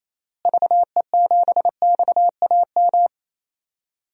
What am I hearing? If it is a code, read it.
Morse code, heard as a keyed tone, about 28 wpm, 700 Hz. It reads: VE7XAM
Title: DXCC-100 Call Signs - 6 Characters - Repeated 3x - 28wpm